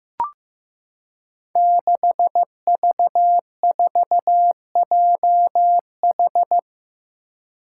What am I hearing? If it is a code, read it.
Morse code, heard as a keyed tone, about 15 wpm, 700 Hz. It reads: 6V4JH
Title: Five Challenging Letter-Number Combinations 15wpm